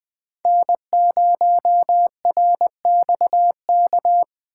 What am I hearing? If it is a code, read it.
Morse code, heard as a keyed tone, about 20 wpm, 700 Hz. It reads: N0RXK
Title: USA Call Signs - 5 Characters - 3x Character Spacing - 20wpm